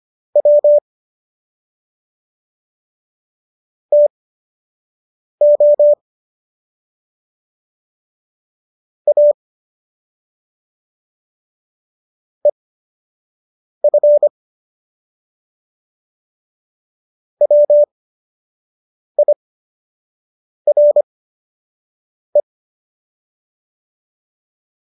Morse code, heard as a keyed tone, about 25 wpm, spaced as for 6 wpm, 590 Hz. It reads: W TO A EF WIRE
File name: qso002 06